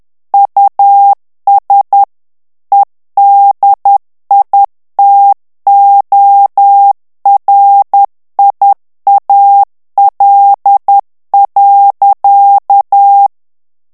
Morse code, heard as a keyed tone, about 10 wpm, 800 Hz. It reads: US EDITORIAL.
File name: ANGLAIS texte clair 3 vitesse 10 mots minute